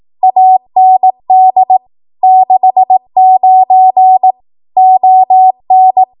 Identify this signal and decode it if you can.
Morse code, heard as a keyed tone, about 18 wpm, 750 Hz. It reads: AND 69 ON
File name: Archive / 18 WPM / 18WPM